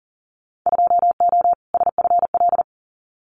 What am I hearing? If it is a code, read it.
Morse code, heard as a keyed tone, about 38 wpm, 700 Hz. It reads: 2O SFL
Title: Winter Field Day Exchanges 40wpm